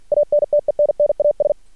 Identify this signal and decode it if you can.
Morse code, heard as a keyed tone, about 48 wpm, 590 Hz. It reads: ANTENNAS
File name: archived / CRSnet